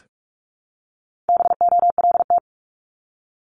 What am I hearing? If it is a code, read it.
Morse code, heard as a keyed tone, about 46 wpm, 700 Hz. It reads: BOLT
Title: Multi-letter Phonemes Repeated 3x 45wpm